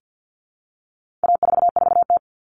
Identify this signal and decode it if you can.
Morse code, heard as a keyed tone, about 50 wpm, 700 Hz. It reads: A44T
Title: All Cut Numbers - 4-Digit 50wpm